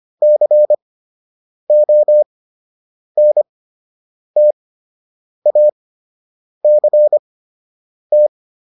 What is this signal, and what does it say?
Morse code, heard as a keyed tone, about 25 wpm, spaced as for 8 wpm, 600 Hz. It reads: CONTACT